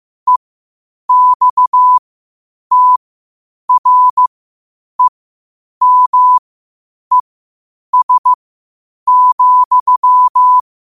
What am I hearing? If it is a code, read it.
Morse code, heard as a keyed tone, about 15 wpm, spaced as for 8 wpm, 1000 Hz. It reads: EXTREMES,
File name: qotdfiles / 8wpm